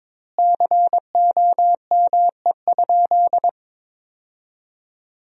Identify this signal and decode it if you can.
Morse code, heard as a keyed tone, about 22 wpm, 700 Hz. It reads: COME?
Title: 4x Word-Spacing - Sentences from Top 200 Words - 22wpm